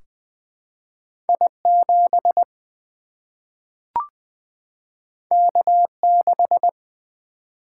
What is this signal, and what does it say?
Morse code, heard as a keyed tone, about 20 wpm, 700 Hz. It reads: I 7 K 6
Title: Single-Letter Single-Number 20wpm